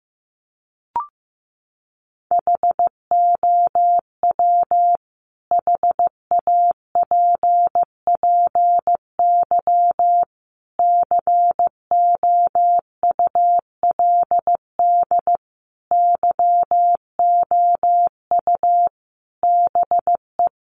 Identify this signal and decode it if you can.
Morse code, heard as a keyed tone, about 14 wpm, 700 Hz. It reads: HOW HAPPY COULD YOU BE
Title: Sentences from Top 1000 Words 15wpm